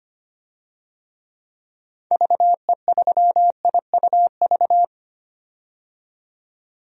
Morse code, heard as a keyed tone, about 25 wpm, 700 Hz. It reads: VE3IUV
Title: DXCC-100 Call Signs - 6 Characters - 25wpm